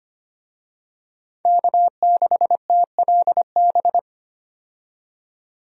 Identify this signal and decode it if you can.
Morse code, heard as a keyed tone, about 25 wpm, 700 Hz. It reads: K6TLB
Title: USA Call Signs - 5 Characters - 25wpm